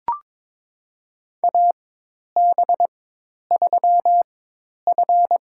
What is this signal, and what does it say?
Morse code, heard as a keyed tone, about 22 wpm, spaced as for 10 wpm, 700 Hz. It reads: AB3F
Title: USA Call Signs - 4 Characters - 4x Character Spacing - 22wpm